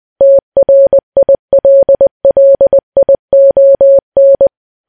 Morse code, heard as a keyed tone, about 20 wpm, 550 Hz. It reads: TRILLION
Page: news-headlines-in-morse-code-at-20-wpm